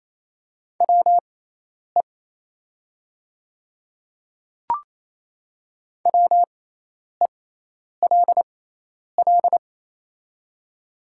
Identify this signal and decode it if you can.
Morse code, heard as a keyed tone, about 28 wpm, spaced as for 10 wpm, 690 Hz. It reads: WE WELL
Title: 6x Character Spacing - Top 300 Words - 28wpm